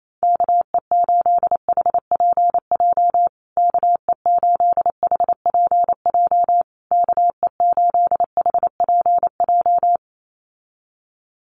Morse code, heard as a keyed tone, about 28 wpm, 700 Hz. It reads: KE8HPJ KE8HPJ KE8HPJ
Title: USA Call Signs Repeated 3x 28wpm